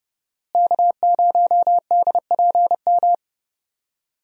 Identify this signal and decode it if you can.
Morse code, heard as a keyed tone, about 32 wpm, 700 Hz. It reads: K0DPM
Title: USA Call Signs - 5 Characters - Repeated 2x - 30wpm